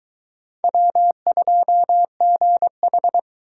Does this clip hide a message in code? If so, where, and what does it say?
Yes: Morse code from 0.6 to 3.2 s, heard as a keyed tone, about 22 wpm, 700 Hz. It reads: W2GH